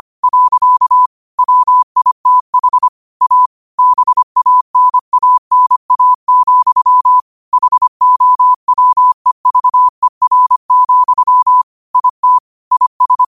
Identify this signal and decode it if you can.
Morse code, heard as a keyed tone, about 25 wpm, 1000 Hz. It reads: . WITH A BANANA, HOWEVER, IT IS